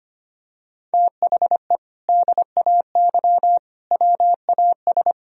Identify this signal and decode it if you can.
Morse code, heard as a keyed tone, about 25 wpm, 700 Hz. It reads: THE DAY WAS
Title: Sentences from Top 1000 Words - Review Part 2 - 25wpm